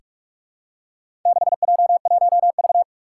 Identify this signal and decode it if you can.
Morse code, heard as a keyed tone, about 44 wpm, 700 Hz. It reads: 6J1V
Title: Four Challenging Letter-Number Combinations 45wpm